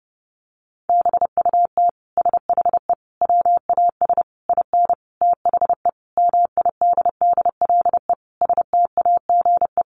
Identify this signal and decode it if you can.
Morse code, heard as a keyed tone, about 30 wpm, 700 Hz. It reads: BUT SHE WAS IN THE MIDDLE STAGE
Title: Sentences from Top 1200 Words 30wpm